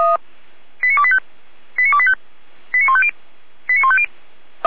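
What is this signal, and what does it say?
Five-tone selective call (ZVEI), heard as quick runs of five tones, tones of about 70 ms. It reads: 89285 89285 89150 89150
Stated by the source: ZVEI five-tone selective call received by SDR.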